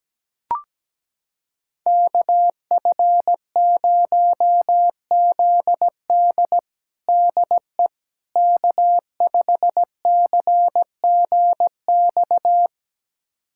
Morse code, heard as a keyed tone, about 17 wpm, 700 Hz. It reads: KF0ZD DE K5CGX
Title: USA Callsign DE Callsign - 5 Characters - 17wpm